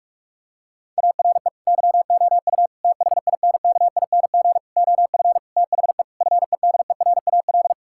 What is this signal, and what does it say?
Morse code, heard as a keyed tone, about 46 wpm, 700 Hz. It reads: ARE YOU THINKING OF THE FEDERAL
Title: Sentences from Top 1000 Words - Review Part 6 - 45wpm